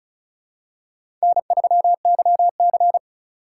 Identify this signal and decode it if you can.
Morse code, heard as a keyed tone, about 34 wpm, 700 Hz. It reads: N3YC
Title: USA Call Signs - 4 Characters - 3x Character Spacing - 35wpm